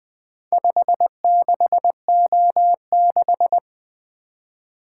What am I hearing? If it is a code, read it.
Morse code, heard as a keyed tone, about 20 wpm, 700 Hz. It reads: 56O6
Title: Four Letter-Number Combinations - 20wpm